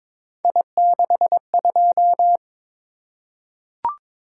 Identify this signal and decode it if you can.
Morse code, heard as a keyed tone, about 22 wpm, 700 Hz. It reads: I62
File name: practice / 22wpm / Three Letter-Number Combinations - 22wpm